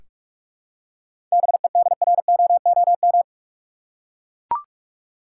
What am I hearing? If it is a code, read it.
Morse code, heard as a keyed tone, about 44 wpm, 700 Hz. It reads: BEDROOM
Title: Seven Letter Words - 45wpm